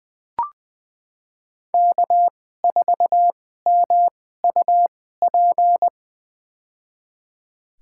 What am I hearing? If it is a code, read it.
Morse code, heard as a keyed tone, about 20 wpm, spaced as for 14 wpm, 700 Hz. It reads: K4MUP
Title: USA Call Signs - 5 Characters - 2x Character Spacing - 20wpm